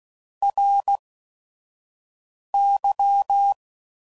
Morse code, heard as a keyed tone, about 16 wpm, spaced as for 5 wpm, 800 Hz. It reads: RY
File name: CW Code Groups @5WPM / Technical Subjects / AND Gate